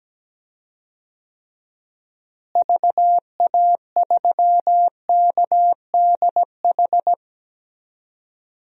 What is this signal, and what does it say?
Morse code, heard as a keyed tone, about 17 wpm, 700 Hz. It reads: VA3KDH
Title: DXCC-100 Call Signs - 6 Characters - Repeated 2x - 17wpm